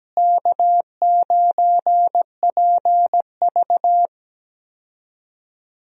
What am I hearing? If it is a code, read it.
Morse code, heard as a keyed tone, about 17 wpm, 700 Hz. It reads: K9PV
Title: USA Call Signs - 4 Characters - Repeated 2x - 17wpm